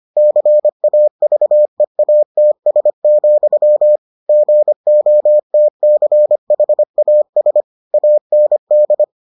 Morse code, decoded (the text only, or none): CAVEATS, GOTCHAS AND